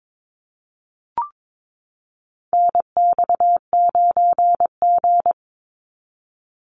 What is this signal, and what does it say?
Morse code, heard as a keyed tone, about 22 wpm, 700 Hz. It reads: NX9G
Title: Call Signs Repeated 3x - 22wpm